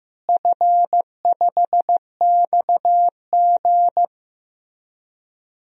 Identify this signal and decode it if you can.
Morse code, heard as a keyed tone, about 15 wpm, 700 Hz. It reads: F5XG
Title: DXCC-100 Call Signs - 4 Characters - Repeated 3x - 15wpm